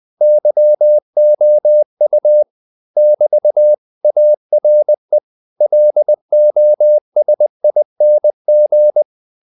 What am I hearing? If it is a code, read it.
Morse code, heard as a keyed tone, about 20 wpm, 600 Hz. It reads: YOU <BT> ARE LOSING
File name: CW / Practice / If20